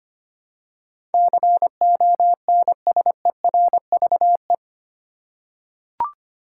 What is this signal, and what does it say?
Morse code, heard as a keyed tone, about 25 wpm, 700 Hz. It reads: CONSERVE
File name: 8 Letter Words - First 3 Letters Given - 25wpm